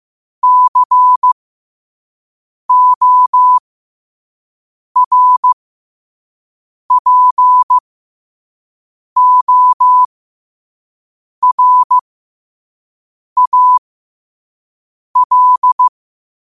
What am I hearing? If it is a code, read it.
Morse code, heard as a keyed tone, about 15 wpm, spaced as for 5 wpm, 1000 Hz. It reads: CORPORAL
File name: qotdfiles / 5wpm